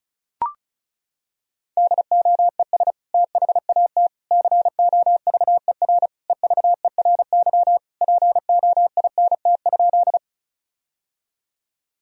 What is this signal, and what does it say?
Morse code, heard as a keyed tone, about 36 wpm, 700 Hz. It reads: DOES THAT COVER EVERY POINT?
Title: Sentences from Top 1000 Words - Review Part 1 - 35wpm